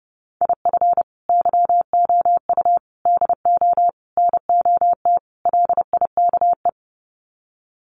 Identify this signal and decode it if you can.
Morse code, heard as a keyed tone, about 30 wpm, 700 Hz. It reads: IF YOU DO NOT LIKE
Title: Sets of 5 Words 30wpm